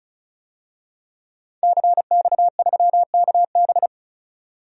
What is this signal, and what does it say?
Morse code, heard as a keyed tone, about 34 wpm, 700 Hz. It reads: CX3KB
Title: DXCC-100 Call Signs - 5 Characters - 35wpm